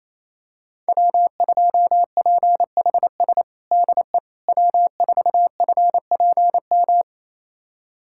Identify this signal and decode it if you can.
Morse code, heard as a keyed tone, about 28 wpm, 700 Hz. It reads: W2PHS DE W4FPM
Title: USA Callsign DE Callsign - 5 Characters - 28wpm